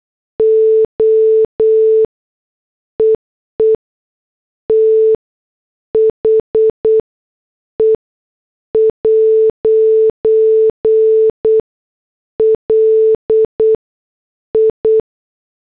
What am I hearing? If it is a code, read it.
Morse code, heard as a keyed tone, about 8 wpm, 440 Hz. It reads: O EE T H E ' L I